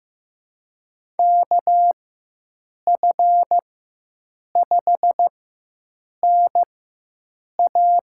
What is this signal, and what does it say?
Morse code, heard as a keyed tone, about 15 wpm, spaced as for 7 wpm, 700 Hz. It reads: KF5NA
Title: USA Call Signs - 4x Character Spacing - 15wpm